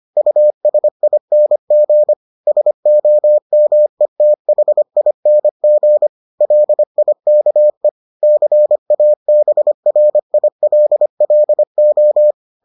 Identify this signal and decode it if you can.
Morse code, heard as a keyed tone, about 25 wpm, 600 Hz. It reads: USING SOMETHING LIKE CABRILLO